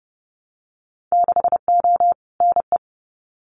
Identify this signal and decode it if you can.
Morse code, heard as a keyed tone, about 30 wpm, 700 Hz. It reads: 6O NE
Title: Winter Field Day Exchanges Encore 30wpm